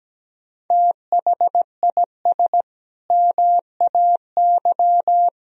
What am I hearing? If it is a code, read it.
Morse code, heard as a keyed tone, about 17 wpm, 700 Hz. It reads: THIS MAY
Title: Sentences from Top 1000 Words - Review Part 3 - 17wpm